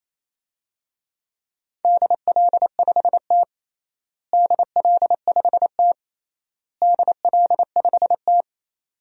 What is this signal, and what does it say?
Morse code, heard as a keyed tone, about 28 wpm, 700 Hz. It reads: DL5T DL5T DL5T
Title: DXCC-100 Call Signs - 4 Characters - Repeated 3x - 28wpm